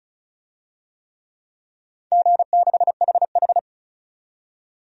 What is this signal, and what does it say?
Morse code, heard as a keyed tone, about 34 wpm, 700 Hz. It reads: G6HH
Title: DXCC-100 Call Signs - 4 Characters - 35wpm